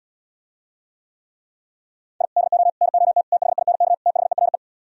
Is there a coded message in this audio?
Yes: Morse code from 2.2 to 4.6 s, heard as a keyed tone, about 78 wpm, 700 Hz. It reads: I DO NOT THINK THERE